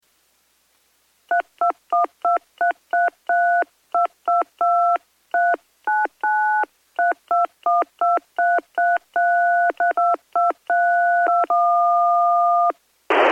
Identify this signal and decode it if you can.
DTMF, heard as telephone touch tones, tones of about 180 ms. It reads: 32123332223993212333322321